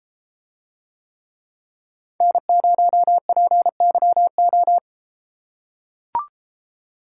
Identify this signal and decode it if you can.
Morse code, heard as a keyed tone, about 33 wpm, 700 Hz. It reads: N0PYO